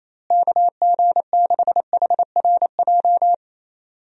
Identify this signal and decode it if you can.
Morse code, heard as a keyed tone, about 28 wpm, 700 Hz. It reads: KG6HRJ
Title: USA Call Signs - 6 Characters - Repeated 3x - 28wpm